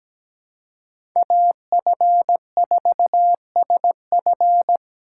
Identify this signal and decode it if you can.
Morse code, heard as a keyed tone, about 17 wpm, 700 Hz. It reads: AF4SF
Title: USA Call Signs - 5 Characters - 4x Character Spacing - 17wpm